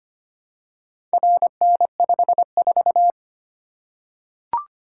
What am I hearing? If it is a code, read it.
Morse code, heard as a keyed tone, about 25 wpm, 700 Hz. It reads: RN54
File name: practice / 25wpm / Four Letter-Number Combinations - 25wpm